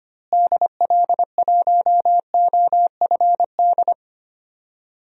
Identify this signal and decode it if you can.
Morse code, heard as a keyed tone, about 25 wpm, 700 Hz. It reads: DL1OFD
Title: DXCC-100 Call Signs - 6 Characters - Repeated 3x - 25wpm